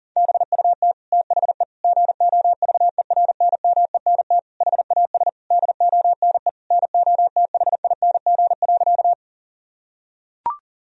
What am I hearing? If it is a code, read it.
Morse code, heard as a keyed tone, about 37 wpm, 700 Hz. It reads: BUT THE GOVERNMENT HAS DONE NOTHING.
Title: Sentences from Top 1000 Words - Review Part 3 - 40wpm